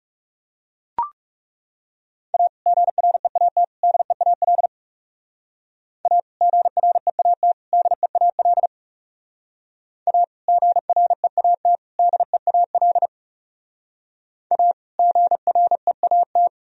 Morse code, heard as a keyed tone, about 37 wpm, 700 Hz. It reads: A GREAT DEAL A GREAT DEAL A GREAT DEAL A GREAT
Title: Speed-Racing - Sets of 3 Words 30wpm